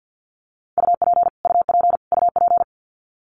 Morse code, heard as a keyed tone, about 50 wpm, 700 Hz. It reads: UP UP UP
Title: Top 200 Words Repeated 50wpm